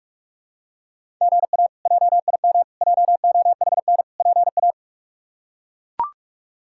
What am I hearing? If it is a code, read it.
Morse code, heard as a keyed tone, about 41 wpm, 700 Hz. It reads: GA JIM JOHN PA
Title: K1USN SST QSO Practice - 45wpm